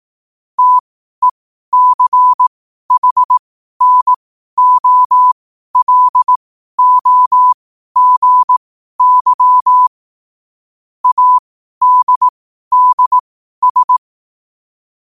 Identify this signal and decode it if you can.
Morse code, heard as a keyed tone, about 18 wpm, spaced as for 12 wpm, 1000 Hz. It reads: TECHNOLOGY ADDS
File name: qotdfiles / 12wpm